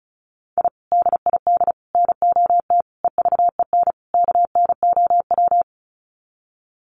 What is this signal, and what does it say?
Morse code, heard as a keyed tone, about 34 wpm, 700 Hz. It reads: I DID NOT EVEN KNOW
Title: Sets of 5 Words Encore 35wpm